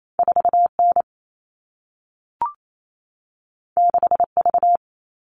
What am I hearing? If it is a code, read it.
Morse code, heard as a keyed tone, about 28 wpm, 700 Hz. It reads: 4N 6V
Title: All Cut Numbers - 2-Digit 28wpm